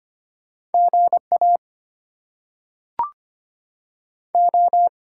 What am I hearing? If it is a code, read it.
Morse code, heard as a keyed tone, about 25 wpm, 700 Hz. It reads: GA O
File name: Top 90 QSO Elements - 8x Character Spacing - 25wpm